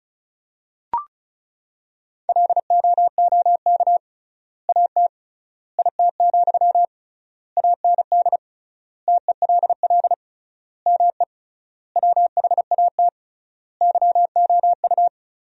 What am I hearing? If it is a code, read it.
Morse code, heard as a keyed tone, about 36 wpm, 700 Hz. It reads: LOOK AT IT, AND TELL ME WHAT YOU
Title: 3x Word-Spacing - Sentences from Top 200 Words - 35wpm